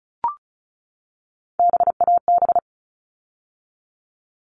Morse code, heard as a keyed tone, about 34 wpm, 700 Hz. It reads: BAB
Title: All Cut Numbers - 3-Digit 35wpm